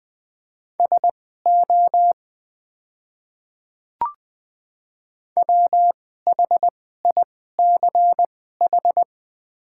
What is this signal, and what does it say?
Morse code, heard as a keyed tone, about 20 wpm, spaced as for 14 wpm, 690 Hz. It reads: SO WHICH
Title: 2x Character Spacing - Top 100 Words - 20wpm